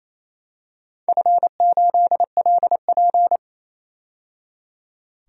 Morse code, heard as a keyed tone, about 27 wpm, 700 Hz. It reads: F8LP
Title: DXCC-100 Call Signs - 4 Characters - 28wpm